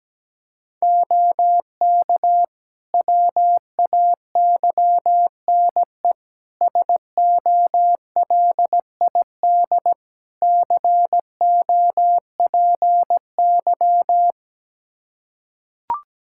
Morse code, heard as a keyed tone, about 17 wpm, 700 Hz. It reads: OK WAYNE SOLID COPY
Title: Practice QSO - Exchange 3 - More Info - 17wpm